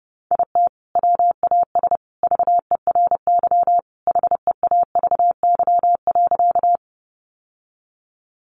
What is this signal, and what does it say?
Morse code, heard as a keyed tone, about 30 wpm, 700 Hz. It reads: IT WAS VERY HEAVY.
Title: Sentences from Top 1100 Words 30wpm